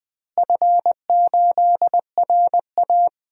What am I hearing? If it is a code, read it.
Morse code, heard as a keyed tone, about 20 wpm, 700 Hz. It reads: F8RA
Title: DXCC-100 Call Signs - 4 Characters - Repeated 2x - 20wpm